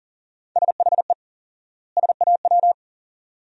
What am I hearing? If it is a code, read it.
Morse code, heard as a keyed tone, about 40 wpm, 700 Hz. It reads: SHE SAW
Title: 4x Word-Spacing - Sentences from Top 300 Words - 40wpm